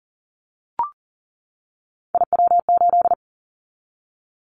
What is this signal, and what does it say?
Morse code, heard as a keyed tone, about 37 wpm, 700 Hz. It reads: IW8
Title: Callsign Prefixes for Top 50 DXCC - 40wpm